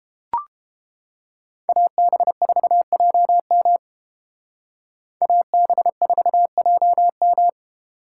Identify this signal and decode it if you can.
Morse code, heard as a keyed tone, about 29 wpm, 700 Hz. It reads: AB4JM AB4JM